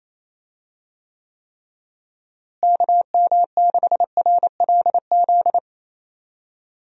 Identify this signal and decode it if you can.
Morse code, heard as a keyed tone, about 28 wpm, 700 Hz. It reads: KM6RLZ
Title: USA Call Signs - 6 Characters - Repeated 3x - 28wpm